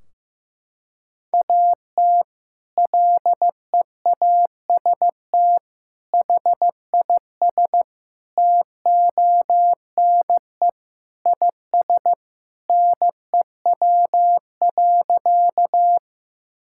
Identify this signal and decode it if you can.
Morse code, heard as a keyed tone, about 15 wpm, 700 Hz. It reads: AT LEAST HIS TONE IS NEW.
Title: Sentences from Top 1600 Words 15wpm